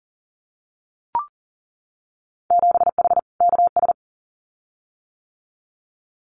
Morse code, heard as a keyed tone, about 40 wpm, 700 Hz. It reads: 7H KS
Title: Winter Field Day Exchanges Encore 40wpm